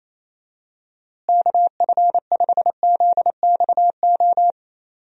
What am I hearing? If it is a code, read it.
Morse code, heard as a keyed tone, about 28 wpm, 700 Hz. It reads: KF5ZXO
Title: USA Call Signs - 6 Characters - Repeated 3x - 28wpm